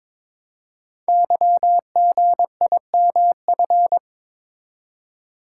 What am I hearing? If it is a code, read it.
Morse code, heard as a keyed tone, about 22 wpm, 700 Hz. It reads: YGIMF
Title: Code Groups - 5 Letter:Numbers - 22wpm